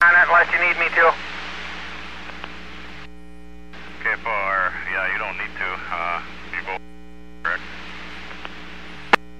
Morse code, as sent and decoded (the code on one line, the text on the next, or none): none